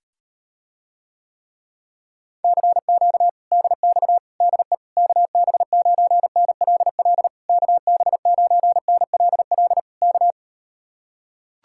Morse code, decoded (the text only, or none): CQ DX DE KB9NLL KB9NLL K